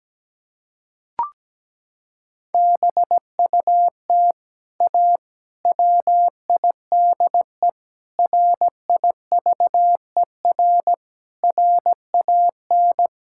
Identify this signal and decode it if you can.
Morse code, heard as a keyed tone, about 17 wpm, 700 Hz. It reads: BUT A WIDE RIVER RAN
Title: Sentences from Top 1000 Words - Review Part 6 - 17wpm